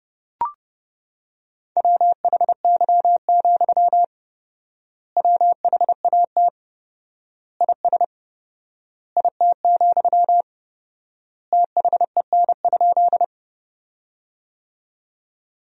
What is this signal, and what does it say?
Morse code, heard as a keyed tone, about 30 wpm, 700 Hz. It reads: WHY, WHAT IS IT, THEN?
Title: 4x Word-Spacing - Sentences from Top 200 Words - 30wpm